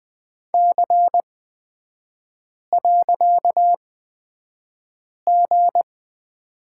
Morse code, heard as a keyed tone, about 20 wpm, spaced as for 5 wpm, 690 Hz. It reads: C.G
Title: 255 Lesson 41e - Period - Three Characters 20wpm